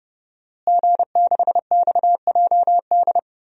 Morse code, heard as a keyed tone, about 32 wpm, 700 Hz. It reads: G6XJD
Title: DXCC-100 Call Signs - 5 Characters - Repeated 2x - 30wpm